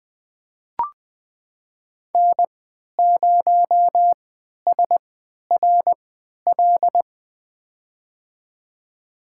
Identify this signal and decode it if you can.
Morse code, heard as a keyed tone, about 20 wpm, spaced as for 11 wpm, 700 Hz. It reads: N0SRL
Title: USA Call Signs - 5 Characters - 3x Character Spacing - 20wpm